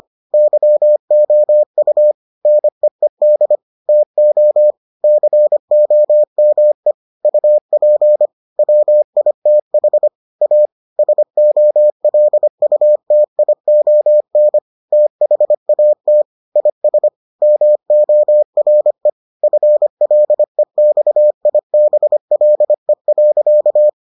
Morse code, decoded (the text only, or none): YOU NEED TO COME UP WITH A SOLUTION THAT IS MORE FLEXIBLE.